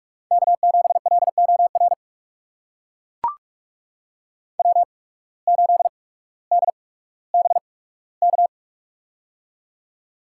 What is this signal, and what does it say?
Morse code, heard as a keyed tone, about 42 wpm, spaced as for 19 wpm, 700 Hz. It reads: K7LOR W8DBK
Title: USA Call Signs - 5 Characters - 8x Character Spacing - 45wpm